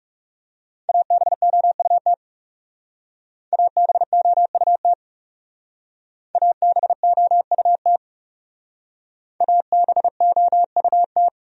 Morse code, heard as a keyed tone, about 38 wpm, 700 Hz. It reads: ABOUT ABOUT ABOUT ABOUT